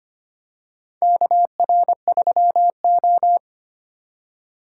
Morse code, heard as a keyed tone, about 25 wpm, 690 Hz. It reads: KR3O
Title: USA Call Signs - 4 Characters - 4x Character Spacing - 25wpm